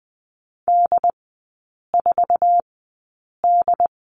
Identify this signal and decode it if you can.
Morse code, heard as a keyed tone, about 20 wpm, spaced as for 8 wpm, 700 Hz. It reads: D4D
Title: 074 Lesson 13e - Letter L - Three Characters 20wpm